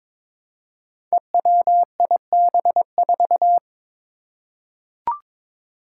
Morse code, heard as a keyed tone, about 22 wpm, 700 Hz. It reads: EWIB4
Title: Five Challenging Letter-Number Combinations 22wpm